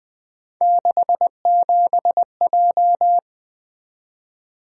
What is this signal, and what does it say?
Morse code, heard as a keyed tone, about 20 wpm, 700 Hz. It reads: 67J
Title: Three Letter-Number Combinations - 20wpm